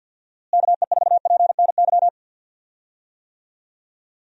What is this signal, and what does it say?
Morse code, heard as a keyed tone, about 46 wpm, 700 Hz. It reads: KE4PNY
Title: USA Call Signs - 6 Characters - Repeated 3x - 50wpm